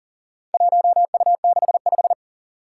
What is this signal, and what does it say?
Morse code, heard as a keyed tone, about 40 wpm, 700 Hz. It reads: 1U65
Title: Four Challenging Letter-Number Combinations 40wpm